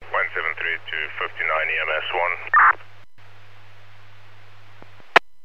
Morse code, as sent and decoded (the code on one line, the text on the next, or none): none